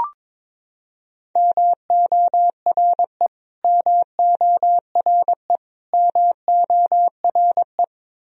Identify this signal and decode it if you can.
Morse code, heard as a keyed tone, about 22 wpm, 700 Hz. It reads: MORE MORE MORE
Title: Top 100 Words Repeated 22wpm